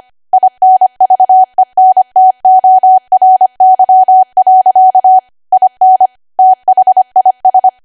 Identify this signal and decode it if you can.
Morse code, heard as a keyed tone, about 25 wpm, 750 Hz. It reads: INVENTORY. IN THIS
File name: cwa / W1AW-Level3 / 25WPM